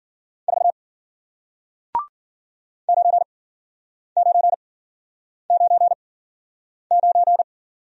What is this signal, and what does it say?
Morse code, heard as a keyed tone, about 49 wpm, spaced as for 9 wpm, 700 Hz. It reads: 4 9999
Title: Speed-Racing Single Letter-Number 40wpm